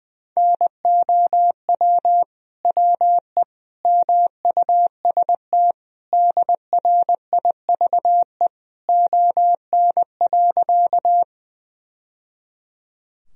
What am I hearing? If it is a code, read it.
Morse code, heard as a keyed tone, about 20 wpm, 700 Hz. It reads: NOW WE MUST DRIVE ON.
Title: Sentences from Top 1000 Words - Review Part 2 - 20wpm